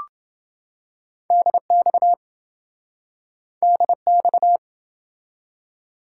Morse code, heard as a keyed tone, about 28 wpm, 700 Hz. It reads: DX DX